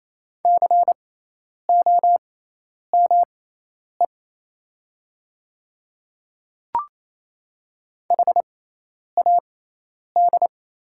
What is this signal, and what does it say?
Morse code, heard as a keyed tone, about 28 wpm, spaced as for 10 wpm, 690 Hz. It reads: COME HAD
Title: 6x Character Spacing - Top 500 Words - 28wpm